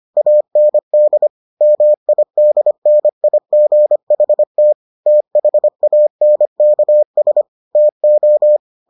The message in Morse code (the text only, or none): AND MIDNIGHT THANKS TO